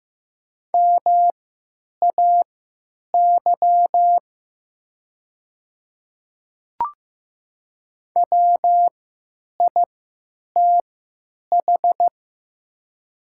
Morse code, heard as a keyed tone, about 15 wpm, spaced as for 9 wpm, 700 Hz. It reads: MAY WITH
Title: 3x Character Spacing - Top 500 Words - 15wpm